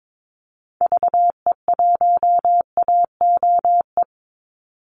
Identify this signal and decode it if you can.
Morse code, heard as a keyed tone, about 22 wpm, 700 Hz. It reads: VE1AOE
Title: Call Signs Repeated 3x - 22wpm